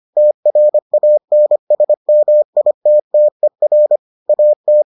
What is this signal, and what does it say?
Morse code, heard as a keyed tone, about 25 wpm, 600 Hz. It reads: TRANSMITTER AT